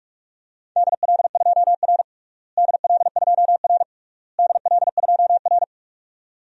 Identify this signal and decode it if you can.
Morse code, heard as a keyed tone, about 41 wpm, 700 Hz. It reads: DL2R DL2R DL2R
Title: DXCC-100 Call Signs - 4 Characters - Repeated 3x - 45wpm